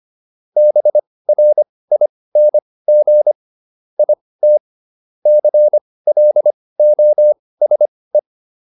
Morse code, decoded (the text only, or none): BRING IT CLOSE